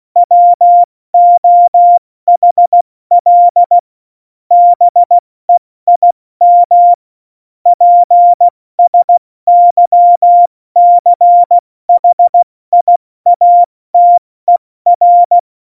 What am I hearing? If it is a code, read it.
Morse code, heard as a keyed tone, about 16 wpm, 700 Hz. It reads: WOHL BEIM PSYCHIATER
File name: Morsekurs / Witze / Witz 28. November 2017, 70 F80